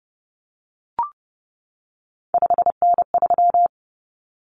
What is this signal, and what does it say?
Morse code, heard as a keyed tone, about 30 wpm, 700 Hz. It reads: 5N3
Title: Common Cut Numbers - 3-Digit 30wpm